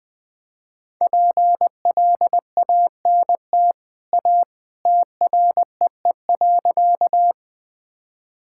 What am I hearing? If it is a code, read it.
Morse code, heard as a keyed tone, about 20 wpm, 700 Hz. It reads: PLANT A TREE.
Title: Easy Sentences with Hints - 20wpm